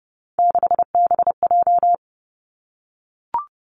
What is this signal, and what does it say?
Morse code, heard as a keyed tone, about 30 wpm, 700 Hz. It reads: 6BJ
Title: Three Challenging Letter-Number Combinations 30wpm